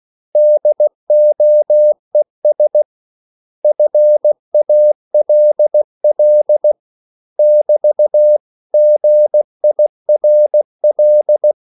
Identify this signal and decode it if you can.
Morse code, heard as a keyed tone, about 16 wpm, 600 Hz. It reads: DOES FALL <BT> GIRL
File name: CW / Practice / 500MostCommonEnglishWords16